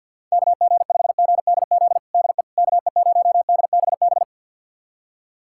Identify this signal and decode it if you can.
Morse code, heard as a keyed tone, about 46 wpm, 700 Hz. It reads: KG5GDZ DE KE0DBB
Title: USA Callsign DE Callsign - 6 Characters - 50wpm